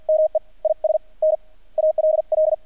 Morse code, heard as a keyed tone, about 48 wpm, spaced as for 33 wpm, 620 Hz. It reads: ME IS NALL
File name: archived / CRSnet